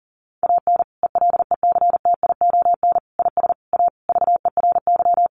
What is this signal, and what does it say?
Morse code, heard as a keyed tone, about 40 wpm, 700 Hz. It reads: AN ELECTION IS A VERY
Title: Sentences from Top 1500 Words 40wpm